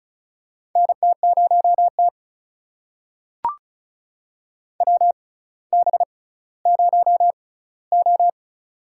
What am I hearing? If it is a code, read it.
Morse code, heard as a keyed tone, about 36 wpm, spaced as for 18 wpm, 700 Hz. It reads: NT0T WB0O
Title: USA Call Signs - 4 Characters - 6x Character Spacing - 35wpm